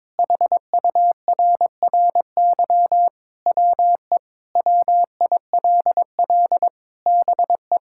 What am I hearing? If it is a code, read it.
Morse code, heard as a keyed tone, about 22 wpm, 700 Hz. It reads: HURRY WE WILL BE
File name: practice / 22wpm / Sentences with 4-Word Phrases - 22wpm